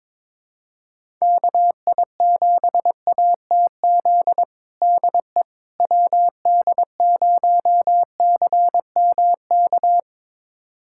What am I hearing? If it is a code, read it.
Morse code, heard as a keyed tone, about 22 wpm, 700 Hz. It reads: KI7ATZ DE WD0CMK
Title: USA Callsign DE Callsign - 6 Characters - 22wpm